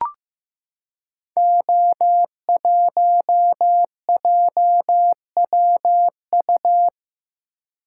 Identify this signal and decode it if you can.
Morse code, heard as a keyed tone, about 15 wpm, 700 Hz. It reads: O1JWU
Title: Code Groups - 5 Letter:Numbers - 15wpm